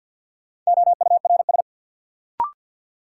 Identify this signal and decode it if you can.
Morse code, heard as a keyed tone, about 49 wpm, 700 Hz. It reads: OURS
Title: Sentences from Top 1300 Words - New Word Drill Alt - 50wpm